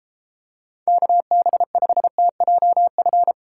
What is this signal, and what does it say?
Morse code, heard as a keyed tone, about 34 wpm, 700 Hz. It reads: KB5TJF